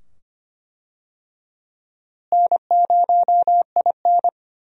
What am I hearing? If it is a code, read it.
Morse code, heard as a keyed tone, about 25 wpm, 700 Hz. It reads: N0IN
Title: USA Call Signs - 4 Characters - Repeated 2x - 25wpm